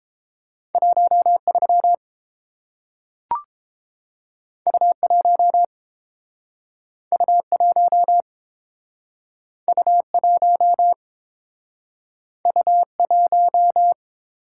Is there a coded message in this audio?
Yes: Morse code from 0.7 to 13.9 s, heard as a keyed tone, about 30 wpm, 700 Hz. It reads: 13 U1 U1 U1 U1